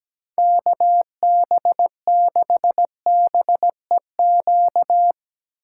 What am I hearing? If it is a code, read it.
Morse code, heard as a keyed tone, about 17 wpm, 700 Hz. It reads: KB6BEQ
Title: USA Call Signs - 6 Characters - 6x Character Spacing - 17wpm